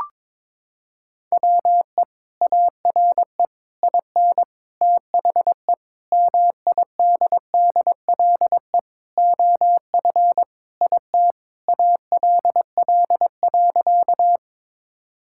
Morse code, heard as a keyed tone, about 23 wpm, 700 Hz. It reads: WE ARE IN THE MIDDLE OF IT ALL.
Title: Sentences from Top 1000 Words - Review Part 6 - 22wpm